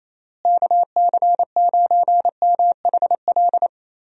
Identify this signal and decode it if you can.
Morse code, heard as a keyed tone, about 28 wpm, 700 Hz. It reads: KC9MHL
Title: USA Call Signs - 6 Characters - Repeated 2x - 28wpm